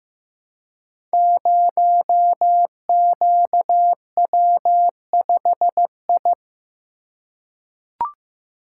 Morse code, heard as a keyed tone, about 15 wpm, 700 Hz. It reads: 0QW5I
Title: Code Groups - 5 Letter:Numbers - 15wpm